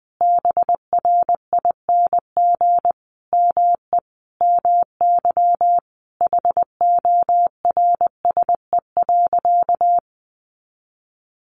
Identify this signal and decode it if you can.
Morse code, heard as a keyed tone, about 20 wpm, 700 Hz. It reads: BRING ME MY HORSE.
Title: Sentences from Top 500 Words 20wpm